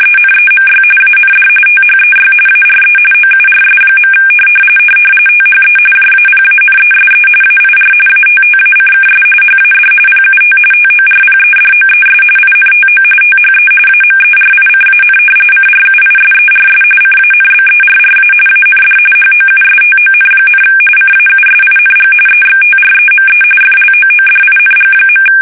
RTTY, about 45.45 baud. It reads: ENEMY AND THEY ARE OURS. THE JAPANESE AIR ATTACKS HAD PUT TORPEDOES INTO THE AMERICAN CRUISERS CANBERRA AND HOUSTON, CAUSING EXTREME FLOODING.
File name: fsk850 / battleshipnj